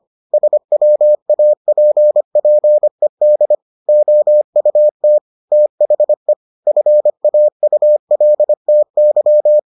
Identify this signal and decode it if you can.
Morse code, heard as a keyed tone, about 25 wpm, 600 Hz. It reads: SWAPPED OUT THE FAULTY